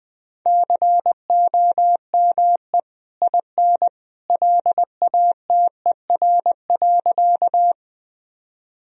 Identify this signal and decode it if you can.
Morse code, heard as a keyed tone, about 19 wpm, 700 Hz. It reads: COME IN LATER.
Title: Sentences from Top 1000 Words - Review Part 4 - 20wpm